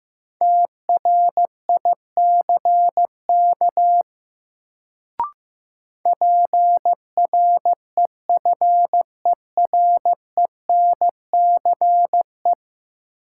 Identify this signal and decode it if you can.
Morse code, heard as a keyed tone, about 15 wpm, 700 Hz. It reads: TRICK PREFERENCE
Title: Sentences from Top 1500 Words - New Word Drill Alt - 15wpm